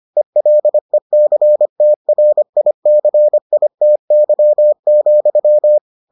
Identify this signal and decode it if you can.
Morse code, heard as a keyed tone, about 25 wpm, 600 Hz. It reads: ELECTRICITY,